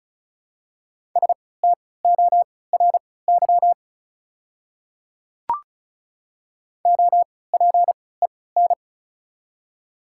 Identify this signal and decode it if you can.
Morse code, heard as a keyed tone, about 36 wpm, spaced as for 20 wpm, 700 Hz. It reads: STORY OPEN
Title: 3x Character Spacing - Top 300 Words - 35wpm